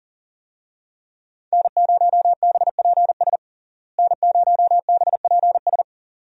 Morse code, heard as a keyed tone, about 40 wpm, 700 Hz. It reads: N0BPS N0BPS
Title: USA Call Signs - 5 Characters - Repeated 2x - 40wpm